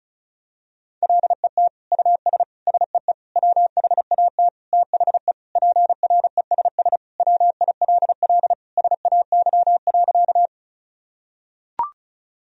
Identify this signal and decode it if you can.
Morse code, heard as a keyed tone, about 36 wpm, 700 Hz. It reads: LET US SEE WHAT THE PRESS WILL SAY.
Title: Sentences from Top 1000 Words - Review Part 1 - 35wpm